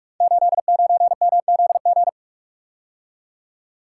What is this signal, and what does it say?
Morse code, heard as a keyed tone, about 44 wpm, 700 Hz. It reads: 89MZG
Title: Code Groups - 5 Letter:Numbers - 45wpm